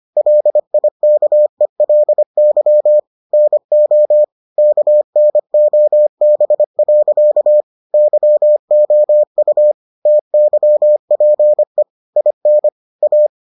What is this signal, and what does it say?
Morse code, heard as a keyed tone, about 25 wpm, 600 Hz. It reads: LIKELY NO KNOB. YOU TYPE IN A